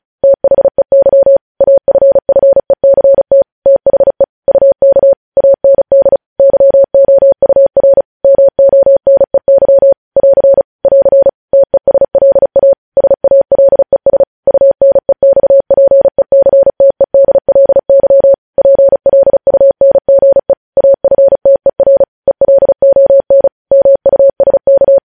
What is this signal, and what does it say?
Morse code, heard as a keyed tone, about 36 wpm, 550 Hz. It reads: THEY AFFECT THE UK AND YOUR MONEY <AR> <AR> TESLA SALES UNEXPECTEDLY PLUNGE AFTER ELON MUSK
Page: news headlines in morse code at 35 wpm